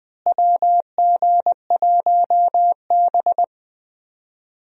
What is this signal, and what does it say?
Morse code, heard as a keyed tone, about 20 wpm, 700 Hz. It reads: WG1B
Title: USA Call Signs - 4 Characters - Repeated 2x - 20wpm